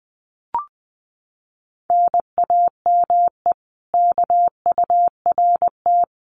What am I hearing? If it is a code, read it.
Morse code, heard as a keyed tone, about 20 wpm, 700 Hz. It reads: NAME KURT
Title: Common QSO Phrases Encore 20wpm